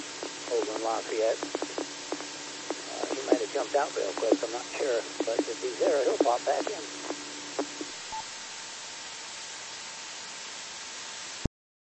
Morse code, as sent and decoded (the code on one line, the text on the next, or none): none